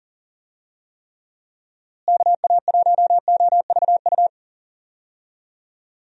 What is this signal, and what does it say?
Morse code, heard as a keyed tone, about 40 wpm, 700 Hz. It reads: KA1OVU
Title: USA Call Signs - 6 Characters - Repeated 3x - 40wpm